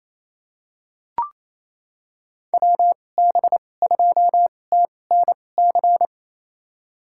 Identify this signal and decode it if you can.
Morse code, heard as a keyed tone, about 28 wpm, spaced as for 20 wpm, 700 Hz. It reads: WB2TNC
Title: USA Call Signs - 6 Characters - 2x Character Spacing - 28wpm